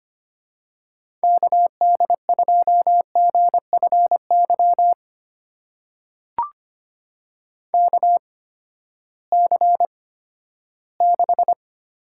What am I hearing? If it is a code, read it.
Morse code, heard as a keyed tone, about 25 wpm, spaced as for 14 wpm, 700 Hz. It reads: KD2GFY KC6
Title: USA Call Signs - 6 Characters - 8x Character Spacing - 25wpm